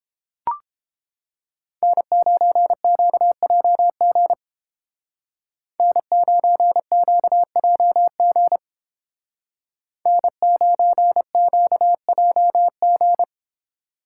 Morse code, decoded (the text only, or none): N9QJG N9QJG N9QJG